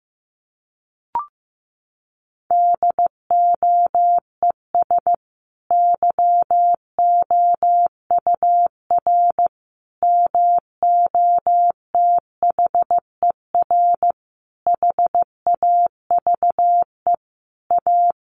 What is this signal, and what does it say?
Morse code, heard as a keyed tone, about 15 wpm, 700 Hz. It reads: DOES YOUR MOTHER HAVE A
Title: Sentences from Top 1400 Words 15wpm